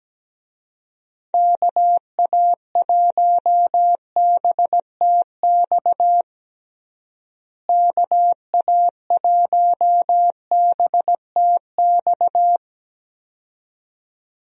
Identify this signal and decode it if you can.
Morse code, heard as a keyed tone, about 17 wpm, 700 Hz. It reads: KA1BTX KA1BTX
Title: USA Call Signs - 6 Characters - Repeated 2x - 17wpm